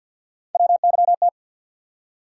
Morse code, heard as a keyed tone, about 47 wpm, 700 Hz. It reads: WYT
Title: USA Call Sign Postfixes - 3 Letters 50wpm